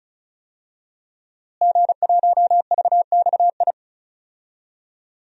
Morse code, heard as a keyed tone, about 34 wpm, 700 Hz. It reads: G1VXI
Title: DXCC-100 Call Signs - 5 Characters - Repeated 3x - 35wpm